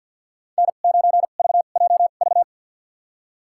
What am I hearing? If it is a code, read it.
Morse code, heard as a keyed tone, about 52 wpm, spaced as for 37 wpm, 700 Hz. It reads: N9VJV
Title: USA Call Signs - 2x Character Spacing - 50wpm